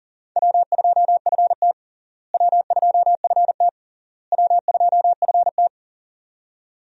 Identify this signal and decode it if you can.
Morse code, heard as a keyed tone, about 40 wpm, 700 Hz. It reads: W2FT W2FT W2FT
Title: USA Call Signs - 4 Characters - Repeated 3x - 40wpm